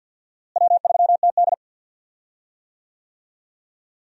Morse code, heard as a keyed tone, about 52 wpm, 700 Hz. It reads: W3TD
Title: USA Call Signs - 4 Characters - Repeated 3x - 50wpm